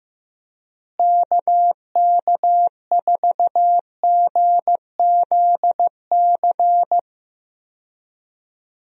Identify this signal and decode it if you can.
Morse code, heard as a keyed tone, about 15 wpm, 700 Hz. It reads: KK4GZC
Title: USA Call Signs - 6 Characters - Repeated 2x - 15wpm